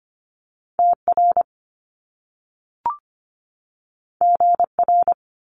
Morse code, heard as a keyed tone, about 25 wpm, 700 Hz. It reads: TR GR
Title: Top Two Letter Combinations in Words 25wpm